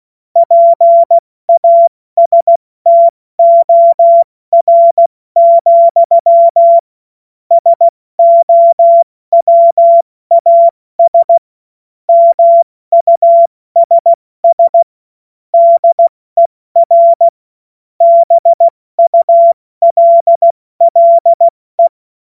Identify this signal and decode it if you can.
Morse code, heard as a keyed tone, about 16 wpm, 680 Hz. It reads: PASTOR, SOWAS MUSS DER BULLE
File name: Witz-29-Juli-2025-70-F80